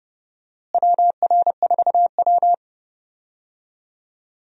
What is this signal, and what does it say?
Morse code, heard as a keyed tone, about 30 wpm, 700 Hz. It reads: WR4W
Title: USA Call Signs - 4 Characters - Repeated 3x - 30wpm